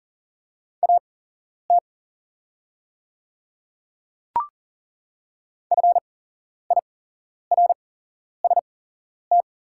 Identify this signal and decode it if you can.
Morse code, heard as a keyed tone, about 40 wpm, spaced as for 11 wpm, 690 Hz. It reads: AT FIRST
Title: 8x Character Spacing - Top 100 Words - 40wpm